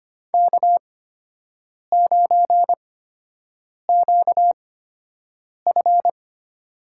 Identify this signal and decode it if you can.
Morse code, heard as a keyed tone, about 25 wpm, spaced as for 7 wpm, 700 Hz. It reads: K9QF
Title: USA Call Signs - 4 Characters - 8x Character Spacing - 25wpm